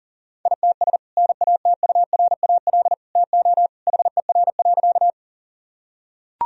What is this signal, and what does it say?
Morse code, heard as a keyed tone, about 40 wpm, 700 Hz. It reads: ITS NATURAL TO HER.
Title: Sentences from Top 1000 Words - Review Part 4 - 40wpm